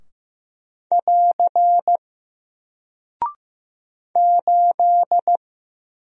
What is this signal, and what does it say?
Morse code, heard as a keyed tone, about 15 wpm, 700 Hz. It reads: <AR> 8
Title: Single Character (Letter:Number:Punctuation:Prosign) 15wpm